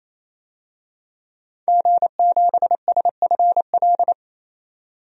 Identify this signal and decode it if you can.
Morse code, heard as a keyed tone, about 28 wpm, 700 Hz. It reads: G7SFL
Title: DXCC-100 Call Signs - 5 Characters - 28wpm